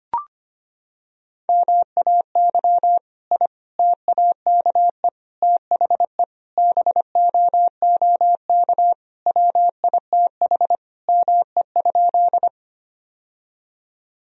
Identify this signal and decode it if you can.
Morse code, heard as a keyed tone, about 25 wpm, 700 Hz. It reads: MAY I TAKE THE BOOK WITH ME?
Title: Sentences from Top 1000 Words - Review Part 1 - 25wpm